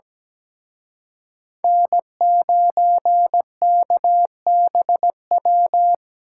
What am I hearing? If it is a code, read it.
Morse code, heard as a keyed tone, about 20 wpm, 700 Hz. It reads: N9KBW
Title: Speed Racing - USA 1x3 Call Signs 17wpm